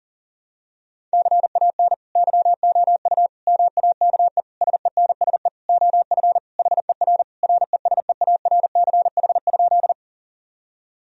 Morse code, heard as a keyed tone, about 40 wpm, 700 Hz. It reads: CAN YOU MAKE SENSE OF HER RESEARCH?
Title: Sentences from Top 1000 Words - Review Part 4 - 40wpm